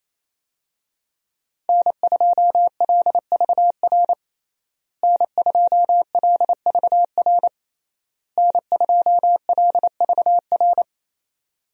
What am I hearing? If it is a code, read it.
Morse code, heard as a keyed tone, about 27 wpm, 700 Hz. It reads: N2LVR N2LVR N2LVR
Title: USA Call Signs - 5 Characters - Repeated 3x - 28wpm